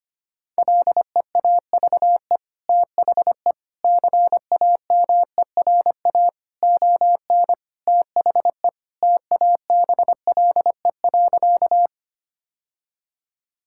Morse code, heard as a keyed tone, about 25 wpm, 700 Hz. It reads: LEAVE THE CAMERA ON THE TABLE.
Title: Sentences from Top 1000 Words - Review Part 5 - 25wpm